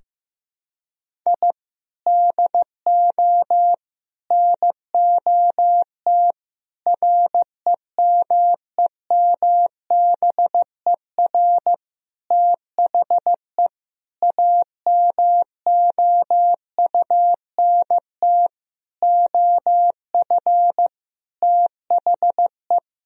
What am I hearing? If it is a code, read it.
Morse code, heard as a keyed tone, about 15 wpm, 700 Hz. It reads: I DO NOT REMEMBER THE AMOUNT OF THE
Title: Sentences from Top 1000 Words - Review Part 4 - 15wpm